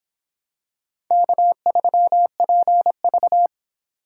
Morse code, heard as a keyed tone, about 25 wpm, 700 Hz. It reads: K3PV